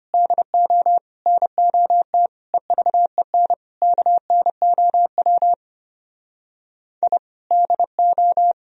Morse code, heard as a keyed tone, about 33 wpm, 700 Hz. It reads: DO NOT EVEN KNOW I DO